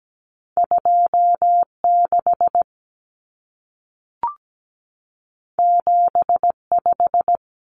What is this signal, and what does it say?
Morse code, heard as a keyed tone, about 17 wpm, 700 Hz. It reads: 26 75
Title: Numbers - 2-Digit 17wpm